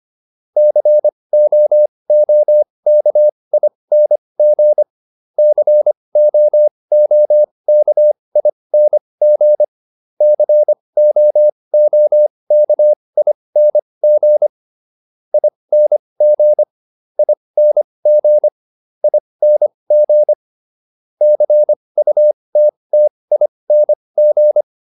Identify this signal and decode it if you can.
Morse code, heard as a keyed tone, about 25 wpm, spaced as for 20 wpm, 600 Hz. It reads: COOKING COOKING COOKING ING ING ING CUTTING